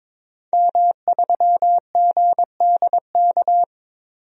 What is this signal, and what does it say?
Morse code, heard as a keyed tone, about 22 wpm, 700 Hz. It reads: M3GDK
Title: DXCC-100 Call Signs - 5 Characters - 22wpm